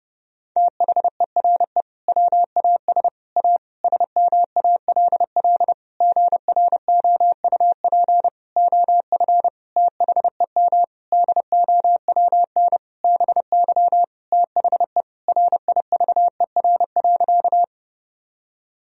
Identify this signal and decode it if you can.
Morse code, heard as a keyed tone, about 30 wpm, 700 Hz. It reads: THERE WAS A SMALL GROUP OF THEM DOWN BY THE RIVER.
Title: Sentences from Top 1000 Words - Review Part 1 - 30wpm